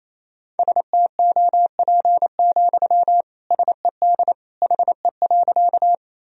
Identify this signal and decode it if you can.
Morse code, heard as a keyed tone, about 28 wpm, 700 Hz. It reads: STOP, SED HE.
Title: Sentences from Top 300 Words and Abbreviations 28wpm